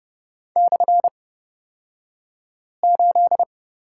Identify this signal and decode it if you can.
Morse code, heard as a keyed tone, about 30 wpm, spaced as for 5 wpm, 700 Hz. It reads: /8
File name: Lesson 37d - Letter x - Two Characters 30wpm